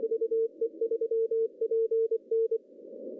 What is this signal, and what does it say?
Morse code, heard as a keyed tone, about 24 wpm, 450 Hz. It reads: VE3PN
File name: CW / cqww-cw-k5tr